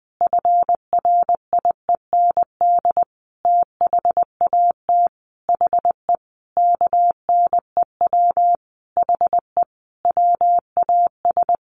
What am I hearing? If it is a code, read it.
Morse code, heard as a keyed tone, about 20 wpm, 700 Hz. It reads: FRIEND THAT HE KNEW HE WAS
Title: Sentences from Top 1300 Words 20wpm